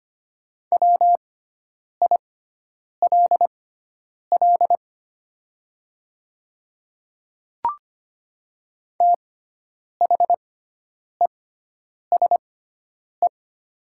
Morse code, heard as a keyed tone, about 25 wpm, spaced as for 9 wpm, 700 Hz. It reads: WILL THESE
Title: 6x Character Spacing - Top 200 Words - 25wpm